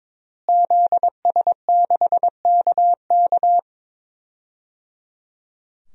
Morse code, heard as a keyed tone, about 21 wpm, 700 Hz. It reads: ZS6KK
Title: DXCC-100 Call Signs - 5 Characters - Repeated 3x - 22wpm